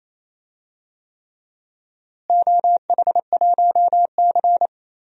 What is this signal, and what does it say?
Morse code, heard as a keyed tone, about 28 wpm, 700 Hz. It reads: OH1C
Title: DXCC-100 Call Signs - 4 Characters - Repeated 3x - 28wpm